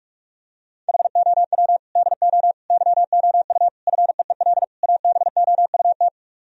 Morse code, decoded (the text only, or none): HOW DO YOU FEEL ABOUT